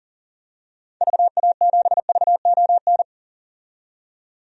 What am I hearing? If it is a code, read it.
Morse code, heard as a keyed tone, about 40 wpm, 700 Hz. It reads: VA7VON
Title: DXCC-100 Call Signs - 6 Characters - 40wpm